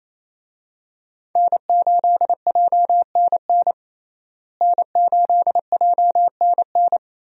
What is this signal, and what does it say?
Morse code, heard as a keyed tone, about 28 wpm, 700 Hz. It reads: N8JNN N8JNN
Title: USA Call Signs - 5 Characters - Repeated 2x - 28wpm